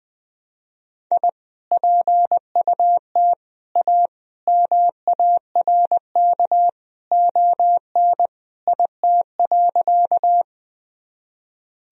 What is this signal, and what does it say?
Morse code, heard as a keyed tone, about 20 wpm, 700 Hz. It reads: I PUT A MARK ON IT.
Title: Sentences from Top 1000 Words - Review Part 1 - 20wpm